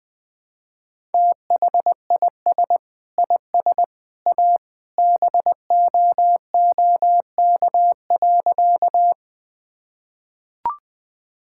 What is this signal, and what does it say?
Morse code, heard as a keyed tone, about 21 wpm, 700 Hz. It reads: THIS IS A BOOK.
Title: Sentences with 3-Word Phrases - 20wpm